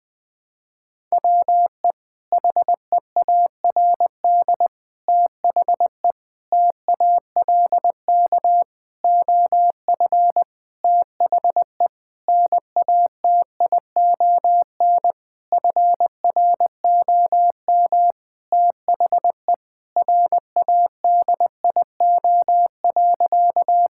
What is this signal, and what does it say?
Morse code, heard as a keyed tone, about 20 wpm, 700 Hz. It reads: WE HEARD THE TALK OF THE NATION FROM THE RADIO.
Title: Sentences with 5-Word Phrases - Spoken Beginning - 20wpm